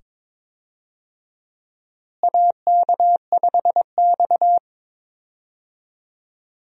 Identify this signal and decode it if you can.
Morse code, heard as a keyed tone, about 22 wpm, 700 Hz. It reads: AK5X
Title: USA Call Signs - 4 Characters - Repeated 2x - 22wpm